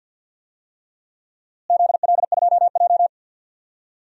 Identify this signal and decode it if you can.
Morse code, heard as a keyed tone, about 52 wpm, 700 Hz. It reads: ZL2J
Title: DXCC-100 Call Signs - 4 Characters - Repeated 3x - 50wpm